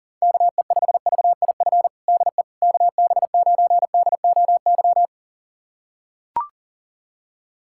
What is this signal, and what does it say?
Morse code, heard as a keyed tone, about 40 wpm, 700 Hz. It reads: KE5VIF DE KB9DOY
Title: USA Callsign DE Callsign - 6 Characters - 40wpm